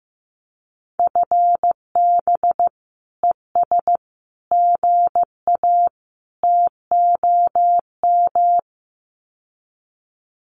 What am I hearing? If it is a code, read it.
Morse code, heard as a keyed tone, about 15 wpm, 700 Hz. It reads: FB ES GA TOM
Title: Common QSO Phrases Encore 15wpm